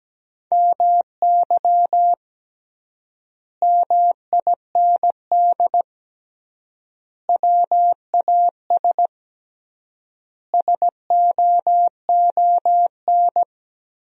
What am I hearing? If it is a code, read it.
Morse code, heard as a keyed tone, about 17 wpm, 700 Hz. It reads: MY MIND WAS SOON
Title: 3x Word-Spacing - Sentences from Top 500 Words - 17wpm